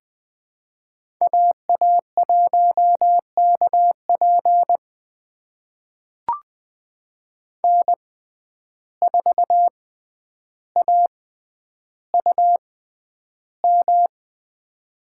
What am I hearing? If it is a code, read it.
Morse code, heard as a keyed tone, about 20 wpm, spaced as for 10 wpm, 700 Hz. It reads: AA1KP N4AUM
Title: USA Call Signs - 5 Characters - 6x Character Spacing - 20wpm